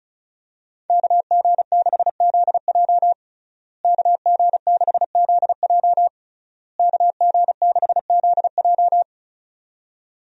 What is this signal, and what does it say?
Morse code, heard as a keyed tone, about 36 wpm, 700 Hz. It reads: KG6ZJ KG6ZJ KG6ZJ
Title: USA Call Signs - 5 Characters - Repeated 3x - 35wpm